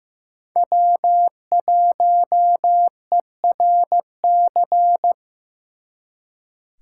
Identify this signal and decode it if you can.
Morse code, heard as a keyed tone, about 18 wpm, 700 Hz. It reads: W1ERC
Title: Speed Racing - USA 1x3 Call Signs 15wpm